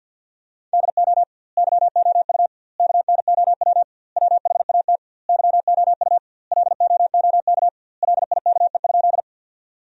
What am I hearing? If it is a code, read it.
Morse code, heard as a keyed tone, about 50 wpm, 700 Hz. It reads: DO YOU KNOW WHAT YOU LOOK LIKE?
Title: 2x Word-Spacing - Sentences from Top 100 Words - 50wpm